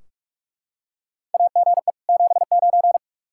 Morse code, heard as a keyed tone, about 42 wpm, 700 Hz. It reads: AGE 79
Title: Practice QSO - Exchange 3 - More Info - 45wpm